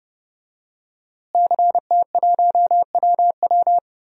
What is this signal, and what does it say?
Morse code, heard as a keyed tone, about 30 wpm, 700 Hz. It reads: CT1WW
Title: DXCC-100 Call Signs - 5 Characters - Repeated 3x - 30wpm